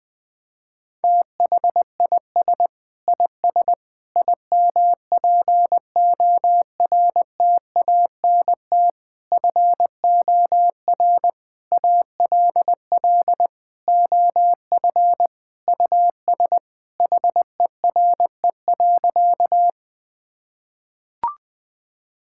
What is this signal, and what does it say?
Morse code, heard as a keyed tone, about 20 wpm, 700 Hz. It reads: THIS IS IMPORTANT FOR ALL OF US HERE.
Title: Sentences with 5-Word Phrases - 20wpm